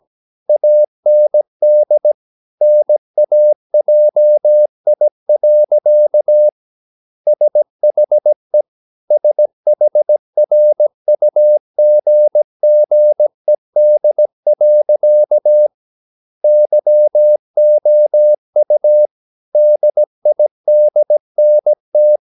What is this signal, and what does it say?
Morse code, heard as a keyed tone, about 17 wpm, 600 Hz. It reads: AND NAJI. SHE SHRUGGED. YOU DIDNT
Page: Romanen Never af Ken Follet på engelsk.